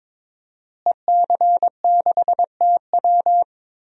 Morse code, heard as a keyed tone, about 22 wpm, 700 Hz. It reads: EC6TW
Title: DXCC-100 Call Signs - 5 Characters - Repeated 2x - 22wpm